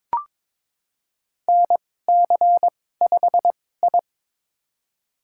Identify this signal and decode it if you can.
Morse code, heard as a keyed tone, about 22 wpm, spaced as for 16 wpm, 700 Hz. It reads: NC5I
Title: USA Call Signs - 4 Characters - 2x Character Spacing - 22wpm